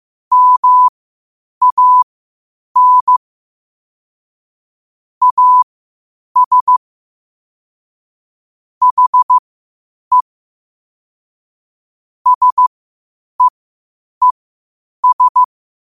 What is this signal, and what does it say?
Morse code, heard as a keyed tone, about 15 wpm, spaced as for 8 wpm, 1000 Hz. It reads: MAN AS HE SEES
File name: qotdfiles / 8wpm